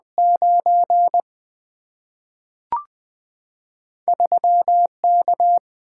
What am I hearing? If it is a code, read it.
Morse code, heard as a keyed tone, about 20 wpm, 700 Hz. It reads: 9 3 K
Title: Single-Letter Single-Number 20wpm